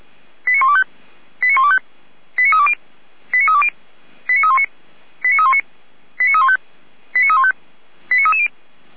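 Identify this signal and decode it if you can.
Five-tone selective call (ZVEI), heard as quick runs of five tones, tones of about 70 ms. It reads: 89125 89125 89320 89320 89319 89319 89315 89315 89330
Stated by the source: ZVEI five-tone selective call 89050 received with an SDR.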